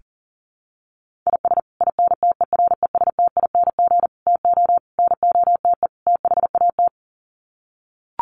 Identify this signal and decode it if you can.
Morse code, heard as a keyed tone, about 37 wpm, 700 Hz. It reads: IS INTERESTING TO NOTE THAT
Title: Sets of 5 Words Encore2 40wpm